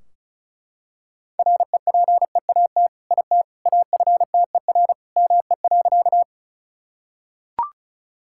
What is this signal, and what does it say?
Morse code, heard as a keyed tone, about 36 wpm, 700 Hz. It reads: REPEAT IT AFTER ME.
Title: Sentences from Top 1600 Words 35wpm